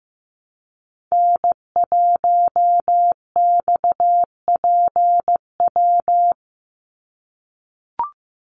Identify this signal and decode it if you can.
Morse code, heard as a keyed tone, about 15 wpm, 700 Hz. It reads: N1XPW
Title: USA Call Signs 15wpm